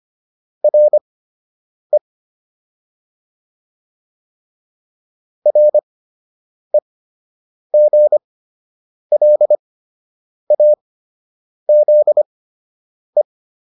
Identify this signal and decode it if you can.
Morse code, heard as a keyed tone, about 25 wpm, spaced as for 8 wpm, 600 Hz. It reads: RE REGLAZE